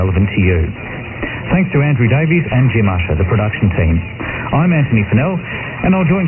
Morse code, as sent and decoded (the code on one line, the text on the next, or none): none